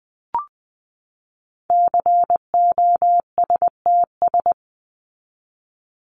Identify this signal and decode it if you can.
Morse code, heard as a keyed tone, about 20 wpm, 700 Hz. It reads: COSTS
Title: ES:S Suffix - Top 100 Words with Base Words 20wpm